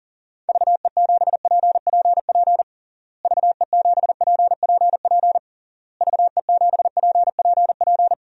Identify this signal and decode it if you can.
Morse code, heard as a keyed tone, about 37 wpm, 700 Hz. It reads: VE7PPP VE7PPP VE7PPP
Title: DXCC-100 Call Signs - 6 Characters - Repeated 3x - 40wpm